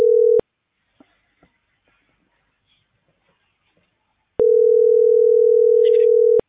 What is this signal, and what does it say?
Ringback tone (North America), 440 + 480 Hz.